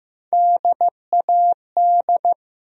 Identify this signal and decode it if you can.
Morse code, heard as a keyed tone, about 15 wpm, 700 Hz. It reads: DAD
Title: USA Call Sign Postfixes - 3 Letters 15wpm